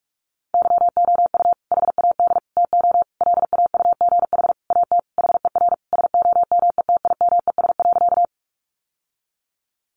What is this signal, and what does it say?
Morse code, heard as a keyed tone, about 44 wpm, 700 Hz. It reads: YOU HAD TO LAUGH AT HER SOMETIMES.
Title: Sentences from Top 600 Words 45wpm